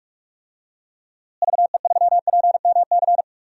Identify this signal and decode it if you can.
Morse code, heard as a keyed tone, about 45 wpm, 700 Hz. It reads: VE3PMC
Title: DXCC-100 Call Signs - 6 Characters - Repeated 3x - 45wpm